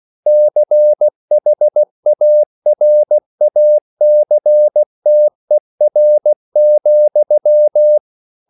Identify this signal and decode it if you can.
Morse code, heard as a keyed tone, about 16 wpm, 600 Hz. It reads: CHARACTER,